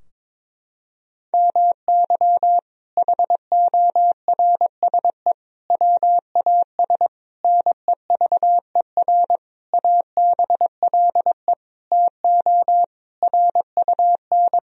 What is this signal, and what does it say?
Morse code, heard as a keyed tone, about 22 wpm, 700 Hz. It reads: MY HORSE WAS NEVER ABLE TO RUN
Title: Sentences from Top 1000 Words - Review Part 2 - 22wpm